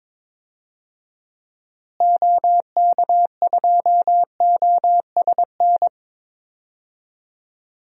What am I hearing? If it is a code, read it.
Morse code, heard as a keyed tone, about 21 wpm, 700 Hz. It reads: OK2OSN
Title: DXCC-100 Call Signs - 6 Characters - Repeated 3x - 22wpm